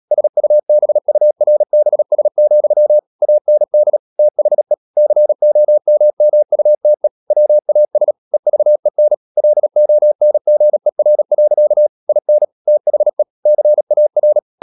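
Morse code, heard as a keyed tone, about 37 wpm, 600 Hz. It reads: SUBURBS, AND THE COMMUTE WAS EVEN LONGER. IN THE CAR
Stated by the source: Morse code practice at 35–39 wpm sending English text of Ken Follett's novel Never, from CW med Gnister.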